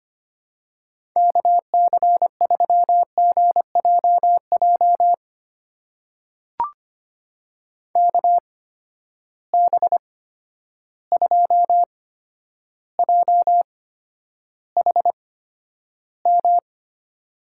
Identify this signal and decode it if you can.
Morse code, heard as a keyed tone, about 25 wpm, spaced as for 11 wpm, 700 Hz. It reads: KC3GJJ KB2JHM
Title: USA Call Signs - 6 Characters - 8x Character Spacing - 25wpm